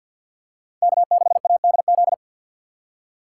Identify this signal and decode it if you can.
Morse code, heard as a keyed tone, about 48 wpm, 700 Hz. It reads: K6ADZ
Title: USA Call Signs - 5 Characters - 3x Character Spacing - 50wpm